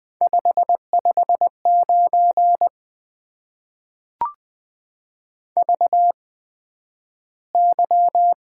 Morse code, heard as a keyed tone, about 20 wpm, spaced as for 11 wpm, 700 Hz. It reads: 559 VY
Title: Top 90 QSO Elements - 8x Character Spacing - 20wpm